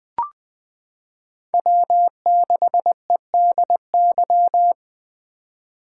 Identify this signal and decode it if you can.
Morse code, heard as a keyed tone, about 20 wpm, 700 Hz. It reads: W6EDY
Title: K1USN SST QSO Practice - 20wpm